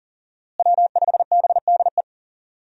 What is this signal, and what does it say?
Morse code, heard as a keyed tone, about 38 wpm, 700 Hz. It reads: W5BDE
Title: USA Call Signs - 5 Characters - Repeated 2x - 40wpm